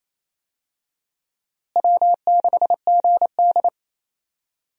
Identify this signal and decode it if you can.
Morse code, heard as a keyed tone, about 28 wpm, 700 Hz. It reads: W6GD
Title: USA Call Signs - 4 Characters - Repeated 2x - 28wpm